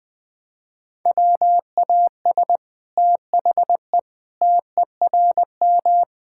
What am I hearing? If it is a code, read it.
Morse code, heard as a keyed tone, about 20 wpm, 700 Hz. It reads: WAS THE TERM
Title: Sentences from Top 1000 Words - Review Part 6 - 20wpm